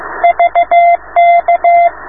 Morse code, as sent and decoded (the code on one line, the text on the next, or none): ...- -.-
VK